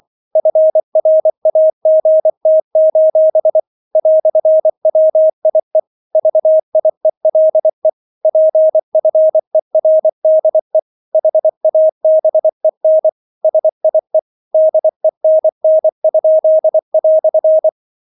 Morse code, heard as a keyed tone, about 24 wpm, 630 Hz. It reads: FRAGT: "WIE VIELE PFERDE HABEN SIE DENN?"